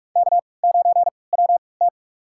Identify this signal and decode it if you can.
Morse code, heard as a keyed tone, about 46 wpm, spaced as for 25 wpm, 700 Hz. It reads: K9WT
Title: USA Call Signs - 4 Characters - 3x Character Spacing - 45wpm